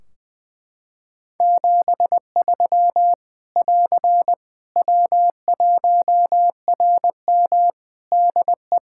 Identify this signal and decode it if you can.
Morse code, heard as a keyed tone, about 21 wpm, 700 Hz. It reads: 73 <AR> W1RM DE
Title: Practice QSO - Exchange 4 - Ending - 20wpm